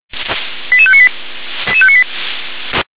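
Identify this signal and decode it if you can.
Five-tone selective call (ZVEI), heard as quick runs of five tones, tones of about 70 ms. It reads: 88598
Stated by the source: ZVEI five-tone call 88839 received by SDR.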